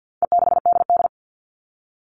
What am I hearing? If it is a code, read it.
Morse code, heard as a keyed tone, about 50 wpm, 700 Hz. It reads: E6DD
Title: All Cut Numbers - 4-Digit 50wpm